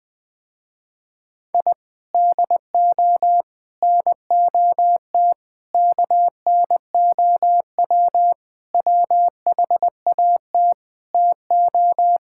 Morse code, heard as a keyed tone, about 20 wpm, 700 Hz. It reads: I DO NOT KNOW WHAT TO
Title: Sentences with 5-Word Phrases - Spoken Beginning - 20wpm